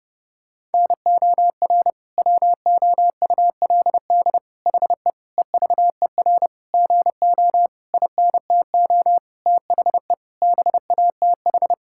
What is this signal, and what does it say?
Morse code, heard as a keyed tone, about 32 wpm, 700 Hz. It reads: NOR WOULD HE EVER GO INTO THE BATH
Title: Sentences from Top 1600 Words 30wpm